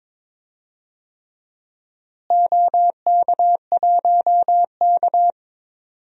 Morse code, heard as a keyed tone, about 22 wpm, 700 Hz. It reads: OK1K
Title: DXCC-100 Call Signs - 4 Characters - Repeated 3x - 22wpm